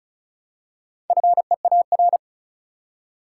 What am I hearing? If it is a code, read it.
Morse code, heard as a keyed tone, about 36 wpm, 700 Hz. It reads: FEAR
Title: Four Letter Words - 35wpm